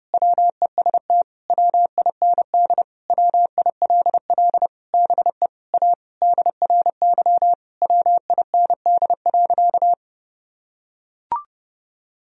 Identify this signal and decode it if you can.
Morse code, heard as a keyed tone, about 30 wpm, 700 Hz. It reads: WEST WIND WILL BE A DRY WIND.
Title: Sentences from Top 1000 Words - Review Part 3 - 30wpm